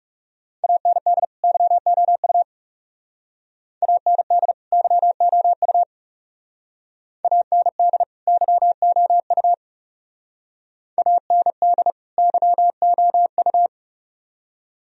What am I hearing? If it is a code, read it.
Morse code, heard as a keyed tone, about 35 wpm, 700 Hz. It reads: AND YOU AND YOU AND YOU AND YOU
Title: Speed-Racing - Sets of 2 Words 30wpm